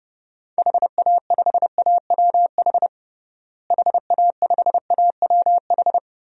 Morse code, heard as a keyed tone, about 32 wpm, 700 Hz. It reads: HA5AWH HA5AWH
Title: DXCC-100 Call Signs - 6 Characters - Repeated 3x - 30wpm